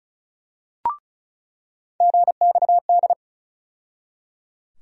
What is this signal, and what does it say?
Morse code, heard as a keyed tone, about 36 wpm, 700 Hz. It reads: GXD
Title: USA Call Sign Postfixes - 3 Letters 35wpm